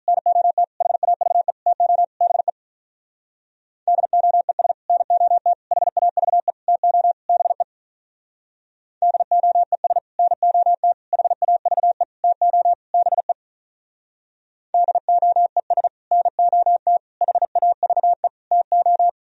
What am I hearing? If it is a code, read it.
Morse code, heard as a keyed tone, about 43 wpm, 700 Hz. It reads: NOT HAVE TO BE DOES NOT HAVE TO BE DOES NOT HAVE TO BE DOES NOT HAVE TO
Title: Speed-Racing - Sets of 5 Words 35wpm